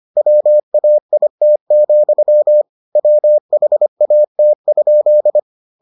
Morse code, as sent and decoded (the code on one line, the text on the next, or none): .-- .- .. - --..-- / .-- .... .- - ..--..
WAIT, WHAT?